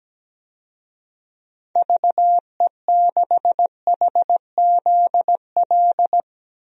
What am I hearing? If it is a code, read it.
Morse code, heard as a keyed tone, about 17 wpm, 700 Hz. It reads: VE6HZL
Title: DXCC-100 Call Signs - 6 Characters - 17wpm